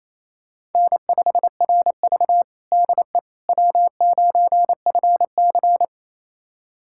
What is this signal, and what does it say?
Morse code, heard as a keyed tone, about 27 wpm, 700 Hz. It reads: N5RV DE W9FC
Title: USA Callsign DE Callsign - 4 Characters - 28wpm